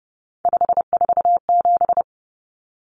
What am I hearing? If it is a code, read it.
Morse code, heard as a keyed tone, about 30 wpm, 700 Hz. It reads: 547
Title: Common Cut Numbers - 3-Digit 30wpm